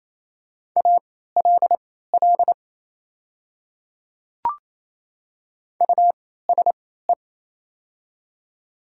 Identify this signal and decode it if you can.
Morse code, heard as a keyed tone, about 28 wpm, spaced as for 16 wpm, 700 Hz. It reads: ALL USE
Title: 3x Character Spacing - Top 300 Words - 28wpm